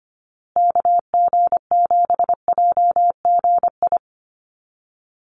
Morse code, heard as a keyed tone, about 25 wpm, 700 Hz. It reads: KG7JGI
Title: USA Call Signs Repeated 3x 25wpm